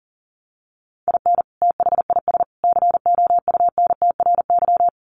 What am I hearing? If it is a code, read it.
Morse code, heard as a keyed tone, about 40 wpm, 700 Hz. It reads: IN THIS COUNTRY
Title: Sets of 3 Words 40wpm